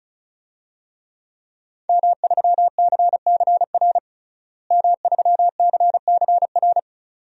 Morse code, heard as a keyed tone, about 36 wpm, 700 Hz. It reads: M3CCR M3CCR
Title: DXCC-100 Call Signs - 5 Characters - Repeated 2x - 35wpm